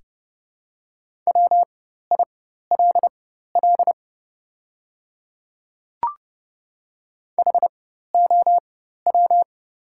Morse code, heard as a keyed tone, about 30 wpm, spaced as for 14 wpm, 700 Hz. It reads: WILL HOW
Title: 4x Character Spacing - Top 100 Words - 30wpm